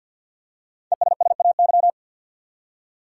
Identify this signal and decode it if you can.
Morse code, heard as a keyed tone, about 51 wpm, 700 Hz. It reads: ESSAY
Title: Five Letter Words - 50wpm